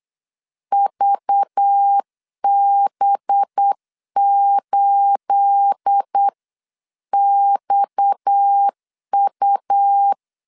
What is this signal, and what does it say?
Morse code, heard as a keyed tone, about 8 wpm, 800 Hz. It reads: VB8XU
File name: CW / DET / 1 9 1500C S C 8wpm 5653